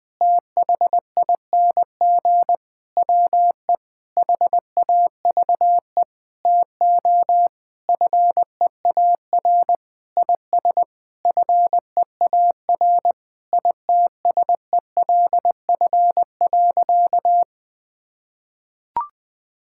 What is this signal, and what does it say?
Morse code, heard as a keyed tone, about 21 wpm, 700 Hz. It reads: THING WE HAVE TO FEAR IS FEAR ITSELF.
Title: Sentences with 9-Word Phrases - 20wpm